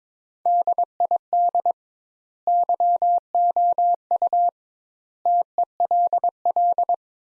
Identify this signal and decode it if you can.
Morse code, heard as a keyed tone, about 22 wpm, 700 Hz. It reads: DID YOU TELL
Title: Sentences Top 200 Words - 22wpm